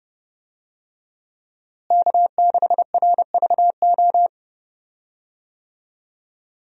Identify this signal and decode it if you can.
Morse code, heard as a keyed tone, about 30 wpm, 700 Hz. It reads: K6RVO
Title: USA Call Signs - 5 Characters - Repeated 3x - 30wpm